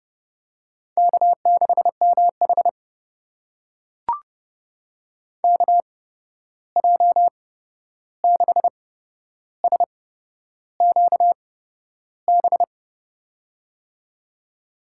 Morse code, heard as a keyed tone, about 31 wpm, spaced as for 11 wpm, 700 Hz. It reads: K6MH KJ6SQB
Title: USA Call Signs - 8x Character Spacing - 30wpm